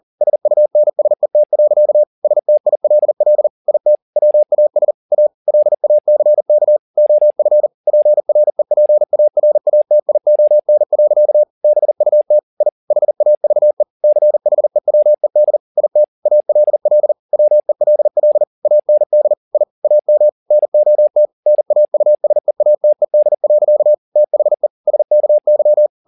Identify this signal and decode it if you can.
Morse code, heard as a keyed tone, about 40 wpm, 600 Hz. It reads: SUNSET. STILL IT WAS A LACK OF PREPARATION. BUT I HAVE CHEWED IT ALL WELL AND I AM NOT NAUSEATED. THE SKY
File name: the old man and the sea 2 40wpm